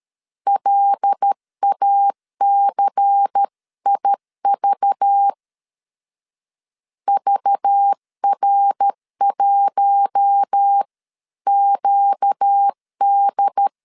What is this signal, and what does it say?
Morse code, heard as a keyed tone, about 13 wpm, 800 Hz. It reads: LACIV VR1QD
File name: CW / DET / 1 6 1500C S C 12wpm 3533